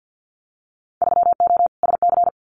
Morse code, heard as a keyed tone, about 50 wpm, 700 Hz. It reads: 3O SC
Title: Winter Field Day Exchanges Encore 50wpm